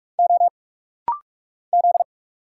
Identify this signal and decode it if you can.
Morse code, heard as a keyed tone, about 46 wpm, 690 Hz. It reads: O Z
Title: Single Letters - Rapid-Fire 45wpm